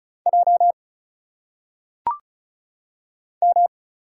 Morse code, heard as a keyed tone, about 34 wpm, 700 Hz. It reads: J M
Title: Single Letters - Phonetic Pronunciation 35wpm